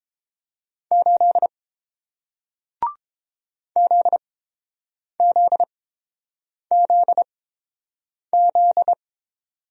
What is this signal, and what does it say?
Morse code, heard as a keyed tone, about 30 wpm, spaced as for 7 wpm, 700 Hz. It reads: 8 ZZZZ